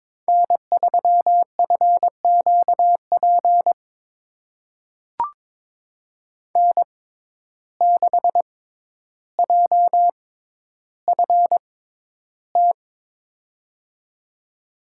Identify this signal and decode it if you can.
Morse code, heard as a keyed tone, about 22 wpm, spaced as for 11 wpm, 700 Hz. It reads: N3FQP N6JFT
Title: USA Call Signs - 5 Characters - 6x Character Spacing - 22wpm